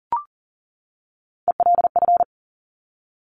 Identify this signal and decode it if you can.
Morse code, heard as a keyed tone, about 38 wpm, 700 Hz. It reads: ELF
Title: Top Three Letter Combinations in Words 40wpm